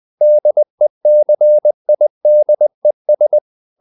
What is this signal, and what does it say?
Morse code, heard as a keyed tone, about 20 wpm, 600 Hz. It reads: DECIDES
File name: the old man and the sea / the old man and the sea part 1 20wpm